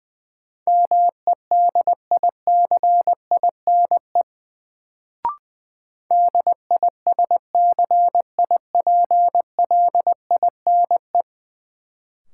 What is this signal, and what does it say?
Morse code, heard as a keyed tone, about 20 wpm, 700 Hz. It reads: MEDICINE DISCIPLINE
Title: Sentences from Top 900 Words - New Word Drill - 20wpm